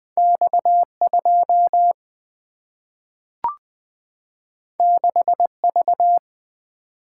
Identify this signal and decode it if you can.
Morse code, heard as a keyed tone, about 20 wpm, 700 Hz. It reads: X2 6V
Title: Single-Letter Single-Number 20wpm